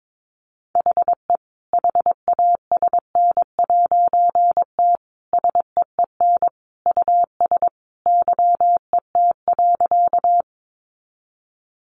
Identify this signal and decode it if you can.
Morse code, heard as a keyed tone, about 22 wpm, 700 Hz. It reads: HE HASN'T SEEN US YET.
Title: A Sound Of Thunder 22wpm